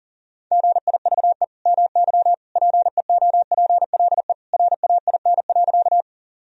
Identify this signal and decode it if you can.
Morse code, heard as a keyed tone, about 40 wpm, 700 Hz. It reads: GIVE MY PEOPLE RAIN.
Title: Sentences from Top 1000 Words - Review Part 1 - 40wpm